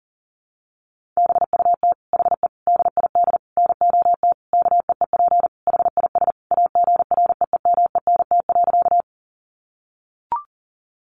Morse code, heard as a keyed tone, about 37 wpm, 700 Hz. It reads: BUT HE DID NOT KEEP HIS AGREEMENT.
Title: Sentences from Top 1500 Words 40wpm